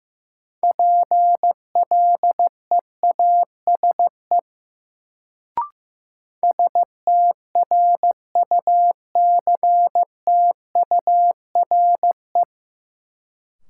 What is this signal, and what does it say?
Morse code, heard as a keyed tone, about 16 wpm, 700 Hz. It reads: PLEASE STRUCTURE
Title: Sentences from Top 1500 Words - New Word Drill Alt - 15wpm